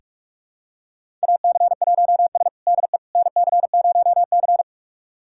Morse code, heard as a keyed tone, about 44 wpm, 700 Hz. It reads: AC1S DE NC0C
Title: USA Callsign DE Callsign - 4 Characters - 45wpm